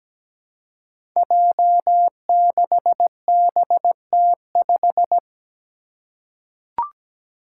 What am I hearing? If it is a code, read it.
Morse code, heard as a keyed tone, about 17 wpm, 700 Hz. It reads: J6BT5
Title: Five Challenging Letter-Number Combinations 17wpm